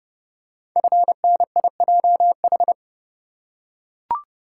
Morse code, heard as a keyed tone, about 30 wpm, 700 Hz. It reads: FNIJH
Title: Code Groups - 5 Letters - 30wpm